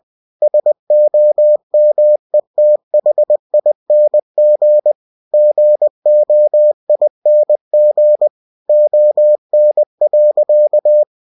Morse code, heard as a keyed tone, about 20 wpm, 600 Hz. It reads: SOMETHING GOING ON.
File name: cw-books / winnie-the-pooh / winnie-20wpm-page-0109